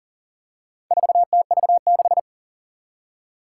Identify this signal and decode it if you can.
Morse code, heard as a keyed tone, about 41 wpm, 700 Hz. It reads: 4TV6
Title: Four Challenging Letter-Number Combinations 40wpm